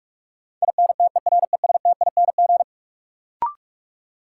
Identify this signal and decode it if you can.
Morse code, heard as a keyed tone, about 46 wpm, 700 Hz. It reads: INTERESTING
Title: Sentences from Top 1000 Words - New Word Drill - 45wpm